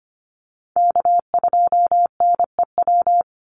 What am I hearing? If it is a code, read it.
Morse code, heard as a keyed tone, about 25 wpm, 700 Hz. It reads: K2NEW
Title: USA Call Signs 25wpm